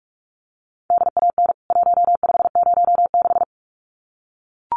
Morse code, heard as a keyed tone, about 44 wpm, 700 Hz. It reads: DAN 1506
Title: CWT Exchange 45wpm